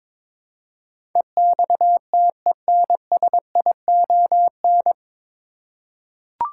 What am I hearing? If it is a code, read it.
Morse code, heard as a keyed tone, about 21 wpm, 700 Hz. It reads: EXTENSION
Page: Morse Code Ninja practice 22wpm